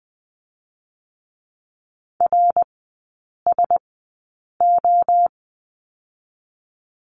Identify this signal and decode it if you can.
Morse code, heard as a keyed tone, about 19 wpm, spaced as for 8 wpm, 690 Hz. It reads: RSO
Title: 087 Lesson 15e - Number 5 - Three Characters 20wpm